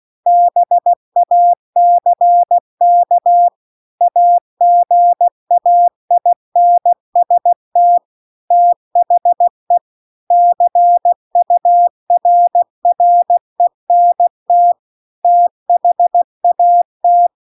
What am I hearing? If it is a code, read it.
Morse code, heard as a keyed tone, about 16 wpm, 700 Hz. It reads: BACK AGAINST THE CURRENT THAT